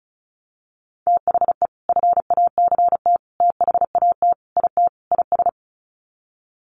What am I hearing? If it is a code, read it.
Morse code, heard as a keyed tone, about 34 wpm, 700 Hz. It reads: THE FACT THAT IT IS
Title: Sets of 5 Words Encore2 35wpm